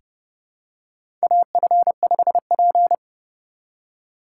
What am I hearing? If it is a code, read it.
Morse code, heard as a keyed tone, about 32 wpm, 700 Hz. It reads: AF5P
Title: K1USN SST QSO Practice - 30wpm